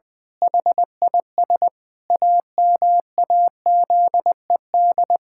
Morse code, heard as a keyed tone, about 21 wpm, 700 Hz. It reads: HIS AMAZED
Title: Pangram Challenge v2 - 20wpm